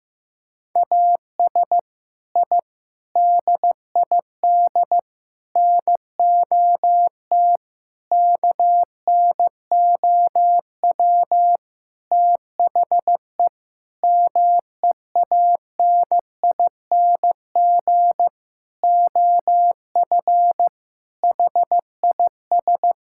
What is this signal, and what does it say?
Morse code, heard as a keyed tone, about 15 wpm, 700 Hz. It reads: AS I DID NOT KNOW THE MEANING OF HIS
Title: Sentences from Top 1000 Words - Review Part 6 - 15wpm